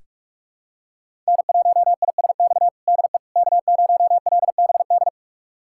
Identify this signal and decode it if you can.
Morse code, heard as a keyed tone, about 41 wpm, 700 Hz. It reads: N1ISX DE K0LBD
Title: USA Callsign DE Callsign - 5 Characters - 45wpm